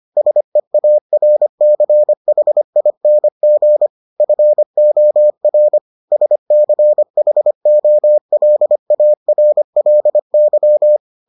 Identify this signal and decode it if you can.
Morse code, heard as a keyed tone, about 25 wpm, 600 Hz. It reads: SEARCHING FOR SCHOLARLY